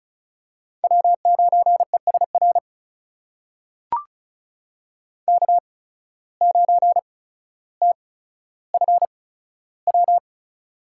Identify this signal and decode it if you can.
Morse code, heard as a keyed tone, about 36 wpm, spaced as for 15 wpm, 700 Hz. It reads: W9ESR K9TFW
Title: USA Call Signs - 5 Characters - 8x Character Spacing - 35wpm